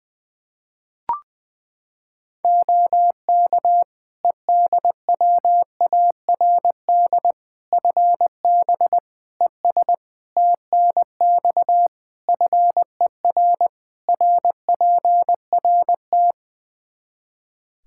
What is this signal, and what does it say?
Morse code, heard as a keyed tone, about 20 wpm, 700 Hz. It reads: OK EDWARD FB ES TNX FER RPRT
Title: Practice QSO - Exchange 2 - Info - 20wpm